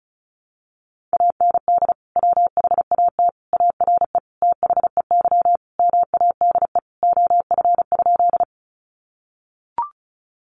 Morse code, heard as a keyed tone, about 34 wpm, 700 Hz. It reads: AND WHAT ARE THEY MADE OF?
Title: Sentences from Top 200 Words 35wpm